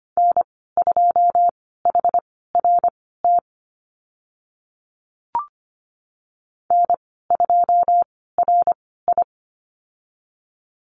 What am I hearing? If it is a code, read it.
Morse code, heard as a keyed tone, about 25 wpm, spaced as for 16 wpm, 700 Hz. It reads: N2HRT N2RI
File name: Lesson 15g - Number 5 - Call Signs 25wpm